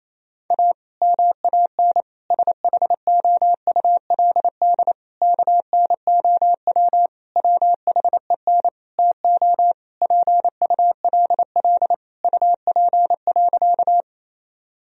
Morse code, heard as a keyed tone, about 28 wpm, 700 Hz. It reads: A MAN SHOULD KNOW WHEN TO PULL UP.
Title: Sentences from Top 1000 Words - Review Part 2 - 28wpm